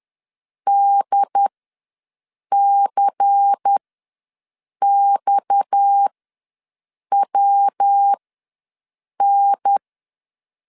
Morse code, heard as a keyed tone, about 10 wpm, spaced as for 6 wpm, 800 Hz. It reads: DCXWN
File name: LEZIONI / 6 PKX0 / 6 PKX ws 10wpm 3706